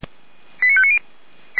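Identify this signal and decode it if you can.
Five-tone selective call (ZVEI), heard as quick runs of five tones, tones of about 70 ms. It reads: 89590
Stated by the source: ZVEI five-tone selective call received on SDR.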